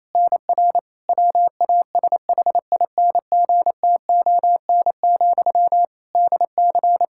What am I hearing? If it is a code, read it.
Morse code, heard as a keyed tone, about 27 wpm, 700 Hz. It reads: NR WASHINGTON, DC
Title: Practice QSO - Exchange 1 - Beginning - 28wpm